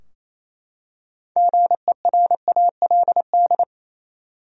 Morse code, heard as a keyed tone, about 28 wpm, 700 Hz. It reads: GERALD
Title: USA Common Names - 28wpm